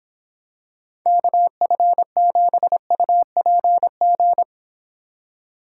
Morse code, heard as a keyed tone, about 26 wpm, 700 Hz. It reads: KF7UPG